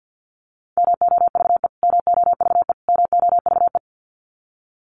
Morse code, heard as a keyed tone, about 50 wpm, 700 Hz. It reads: MOVE MOVE MOVE
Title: Top 500 Words Repeated 50wpm